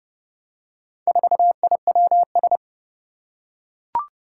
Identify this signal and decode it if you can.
Morse code, heard as a keyed tone, about 30 wpm, 700 Hz. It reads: 4IWS
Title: Four Challenging Letter-Number Combinations 30wpm